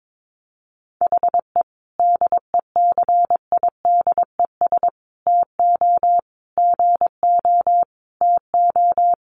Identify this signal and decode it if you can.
Morse code, heard as a keyed tone, about 22 wpm, 700 Hz. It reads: HE DECIDES TO GO TO
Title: Additional Easy Phrases 22wpm